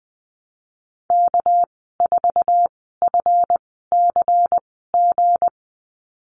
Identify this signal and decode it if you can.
Morse code, heard as a keyed tone, about 20 wpm, spaced as for 14 wpm, 700 Hz. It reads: K4FCG
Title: 213 Lesson 34g - Number 8 - Call Signs 20wpm